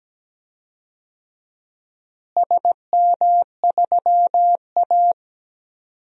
Morse code, heard as a keyed tone, about 17 wpm, 700 Hz. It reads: SM3A
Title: DXCC-100 Call Signs - 4 Characters - Repeated 2x - 17wpm